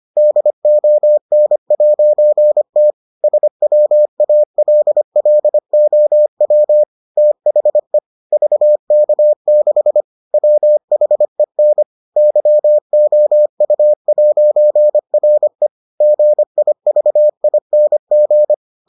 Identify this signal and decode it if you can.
Morse code, heard as a keyed tone, about 25 wpm, 600 Hz. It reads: DON'T SWALLOW THE VK6 WHEN YOU'RE GIVING